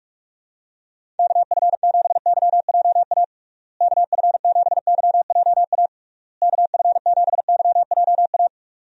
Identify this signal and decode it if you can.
Morse code, heard as a keyed tone, about 48 wpm, 700 Hz. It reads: KF7YJA KF7YJA KF7YJA
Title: USA Call Signs - 6 Characters - Repeated 3x - 45wpm